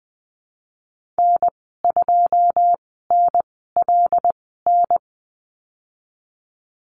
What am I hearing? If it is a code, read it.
Morse code, heard as a keyed tone, about 20 wpm, spaced as for 15 wpm, 700 Hz. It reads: N2NLN
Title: 088 Lesson 15g - Number 5 - Call Signs 20wpm